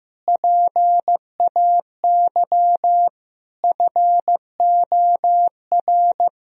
Morse code, heard as a keyed tone, about 15 wpm, 700 Hz. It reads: PAY FOR
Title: Sentences from Top 1000 Words - Review Part 4 - 15wpm